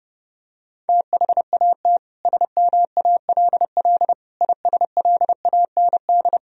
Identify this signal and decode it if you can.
Morse code, heard as a keyed tone, about 30 wpm, 700 Hz. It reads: THAT SMALL ISLAND
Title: Sentences from Top 1000 Words - Review Part 4 - 30wpm